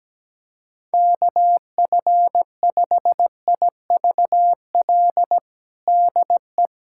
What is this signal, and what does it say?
Morse code, heard as a keyed tone, about 17 wpm, 700 Hz. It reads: KF5IVL DE
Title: USA Callsign DE Callsign - 6 Characters - 17wpm